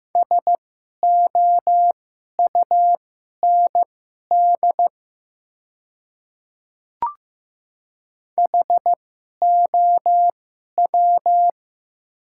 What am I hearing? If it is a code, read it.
Morse code, heard as a keyed tone, about 15 wpm, spaced as for 11 wpm, 700 Hz. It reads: SOUND HOW
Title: 2x Character Spacing - Top 200 Words - 15wpm